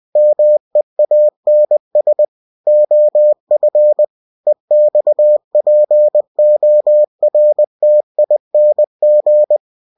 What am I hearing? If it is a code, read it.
Morse code, heard as a keyed tone, about 20 wpm, 600 Hz. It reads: MEANS OF EXPORTING
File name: cw-books / winnie-the-pooh / winnie-20wpm-page-0116